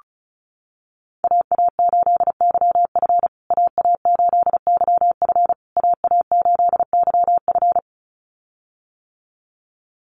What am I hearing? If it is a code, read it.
Morse code, heard as a keyed tone, about 34 wpm, 700 Hz. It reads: AA8YF AA8YF AA8YF
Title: USA Call Signs Repeated 3x 35wpm